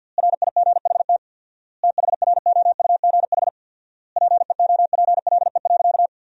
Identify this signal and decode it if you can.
Morse code, heard as a keyed tone, about 50 wpm, 700 Hz. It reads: RIGHT THROUGH PEOPLE.
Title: 4x Word-Spacing - Sentences from Top 200 Words - 50wpm